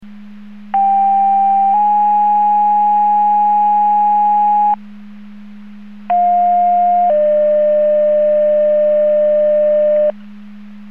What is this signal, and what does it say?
Two-tone page, heard as a one-second tone, then a three-second tone, about 790 then 830 Hz, about 710 then 600 Hz. It is heard from 0.7 to 10.1 s.